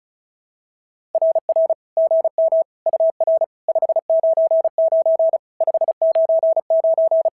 Morse code, heard as a keyed tone, about 36 wpm, 660 Hz. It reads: RR GM UR 599 599
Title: SOTA QSO Practice 35wpm